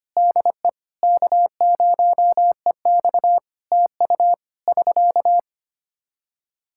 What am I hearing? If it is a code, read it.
Morse code, heard as a keyed tone, about 25 wpm, 700 Hz. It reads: DE K0EX TU <SK>
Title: Practice QSO - Exchange 4 - Ending - 25wpm